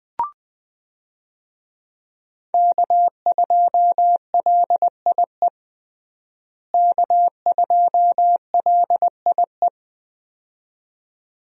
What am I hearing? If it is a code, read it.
Morse code, heard as a keyed tone, about 21 wpm, 700 Hz. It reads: K2LIE K2LIE
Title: USA Call Signs - 5 Characters - Repeated 2x - 20wpm